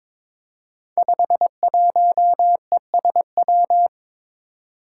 Morse code, heard as a keyed tone, about 22 wpm, 700 Hz. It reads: 51ESW
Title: Five Challenging Letter-Number Combinations 22wpm